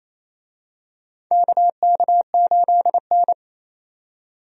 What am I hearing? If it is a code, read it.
Morse code, heard as a keyed tone, about 28 wpm, 700 Hz. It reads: KK8N
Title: USA Call Signs - 4 Characters - 28wpm